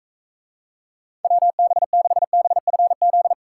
Morse code, heard as a keyed tone, about 44 wpm, 700 Hz. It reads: WB6BFZ